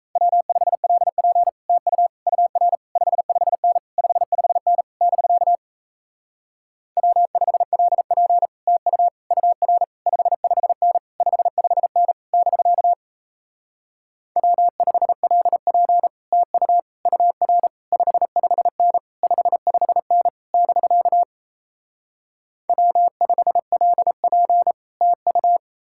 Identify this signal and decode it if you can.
Morse code, heard as a keyed tone, about 38 wpm, 700 Hz. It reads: W5LP TU UR 55N 55N <BK> W5LP TU UR 55N 55N <BK> W5LP TU UR 55N 55N <BK> W5LP TU